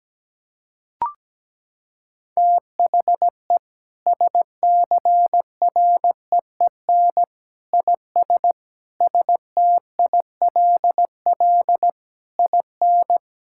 Morse code, heard as a keyed tone, about 17 wpm, 700 Hz. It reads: THE SCREEN IS STILL IN
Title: Sentences from Top 1000 Words - Review Part 6 - 17wpm